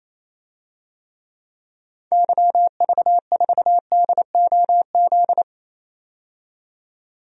Morse code, heard as a keyed tone, about 28 wpm, 700 Hz. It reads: YV4DOZ
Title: DXCC-100 Call Signs - 6 Characters - 28wpm